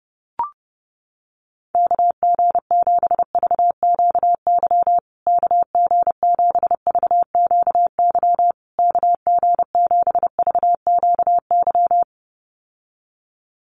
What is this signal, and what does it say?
Morse code, heard as a keyed tone, about 30 wpm, 700 Hz. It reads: KG7VQY KG7VQY KG7VQY
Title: USA Call Signs Repeated 3x 30wpm